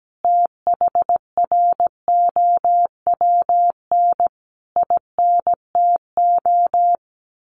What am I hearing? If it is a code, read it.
Morse code, heard as a keyed tone, about 17 wpm, 700 Hz. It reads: THROWN INTO
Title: A Sound Of Thunder 17wpm